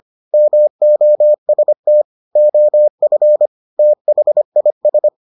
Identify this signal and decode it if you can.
Morse code, heard as a keyed tone, about 25 wpm, 600 Hz. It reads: MOST OF THIS